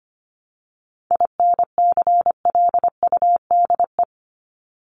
Morse code, heard as a keyed tone, about 25 wpm, 700 Hz. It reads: INCLUDE
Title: ED Suffix - Top 100 Words with Base Words 25wpm